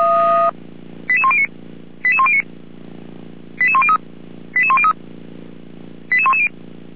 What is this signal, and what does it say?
Five-tone selective call (ZVEI), heard as quick runs of five tones, tones of about 70 ms. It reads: 88109 88109 88103 88103 88110
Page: ZVEI